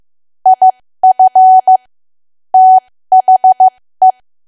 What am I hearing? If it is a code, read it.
Morse code, heard as a keyed tone, about 15 wpm, spaced as for 13 wpm, 750 Hz. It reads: IF THE
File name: cwa / W1AW-Level2 / 13WPM